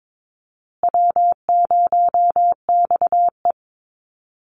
Morse code, heard as a keyed tone, about 22 wpm, 700 Hz. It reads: W0XE
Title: Easy Call Signs Repeated 3x - 22wpm